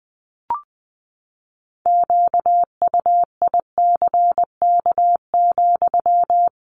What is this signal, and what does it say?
Morse code, heard as a keyed tone, about 20 wpm, 700 Hz. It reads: QUICK,
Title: Sentences from Top 500 Words 20wpm